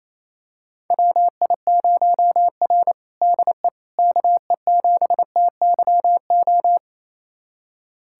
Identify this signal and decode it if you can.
Morse code, heard as a keyed tone, about 28 wpm, 700 Hz. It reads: WI0R DE KE7TYO
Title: USA Callsign DE Callsign 28wpm